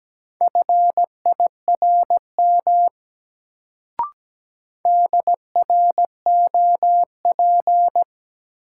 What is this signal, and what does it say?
Morse code, heard as a keyed tone, about 17 wpm, 700 Hz. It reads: FIRM DROP
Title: Sentences from Top 1400 Words - New Word Drill Alt - 17wpm